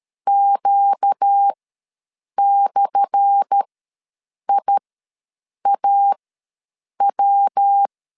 Morse code, heard as a keyed tone, about 13 wpm, spaced as for 7 wpm, 800 Hz. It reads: Q/IAW
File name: LEZIONI / 10 SIGNS / 10D COMA ws 12wpm 3618